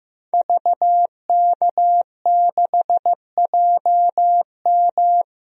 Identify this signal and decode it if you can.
Morse code, heard as a keyed tone, about 16 wpm, 700 Hz. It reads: VK6JM
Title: DXCC-100 Call Signs - 5 Characters - Repeated 2x - 15wpm